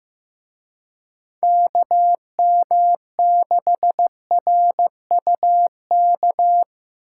Morse code, heard as a keyed tone, about 18 wpm, 700 Hz. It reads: KM6RUK
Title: Speed Racing - USA Call Signs 15wpm